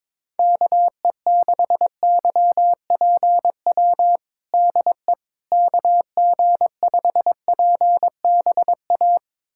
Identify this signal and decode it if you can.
Morse code, heard as a keyed tone, about 21 wpm, 700 Hz. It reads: KE6YPW DE KG5PBA
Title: USA Callsign DE Callsign - 6 Characters - 22wpm